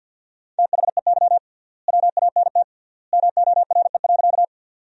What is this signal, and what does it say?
Morse code, heard as a keyed tone, about 50 wpm, 700 Hz. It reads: THEY WANT MORE.
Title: 3x Word-Spacing - Sentences from Top 200 Words - 50wpm